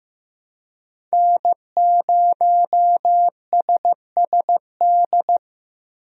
Morse code, heard as a keyed tone, about 15 wpm, 700 Hz. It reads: N0SSD
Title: USA Call Signs - 5 Characters - Repeated 3x - 15wpm